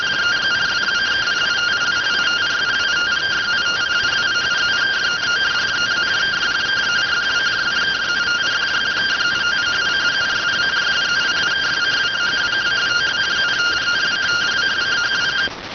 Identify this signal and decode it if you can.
Two-tone teleprinter signal, about 170 Hz shift.